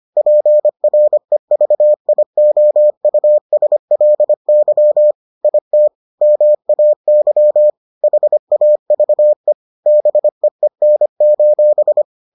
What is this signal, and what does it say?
Morse code, heard as a keyed tone, about 25 wpm, 600 Hz. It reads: PREVIOUSLY IT MAY HAVE BEEN: